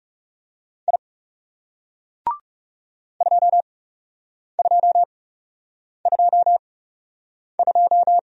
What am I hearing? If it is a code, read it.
Morse code, heard as a keyed tone, about 41 wpm, spaced as for 8 wpm, 700 Hz. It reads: I 2222